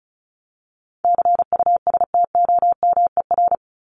Morse code, heard as a keyed tone, about 34 wpm, 700 Hz. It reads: CUSTOMER
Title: Top 1001-1250 Words 35wpm